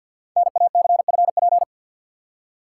Morse code, heard as a keyed tone, about 51 wpm, 700 Hz. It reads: NACFP
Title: Code Groups - 5 Letter:Numbers - 50wpm